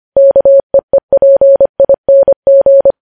Morse code, heard as a keyed tone, about 25 wpm, 550 Hz. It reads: KEEPING